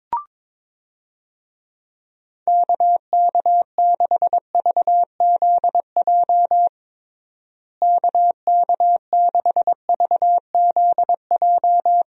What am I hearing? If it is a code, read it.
Morse code, heard as a keyed tone, about 22 wpm, 700 Hz. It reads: KK6VZJ KK6VZJ
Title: USA Call Signs - 6 Characters - Repeated 2x - 22wpm